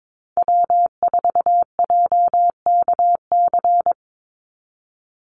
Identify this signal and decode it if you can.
Morse code, heard as a keyed tone, about 22 wpm, 700 Hz. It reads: W4JKC
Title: NA SOTA Chaser Callsigns 22wpm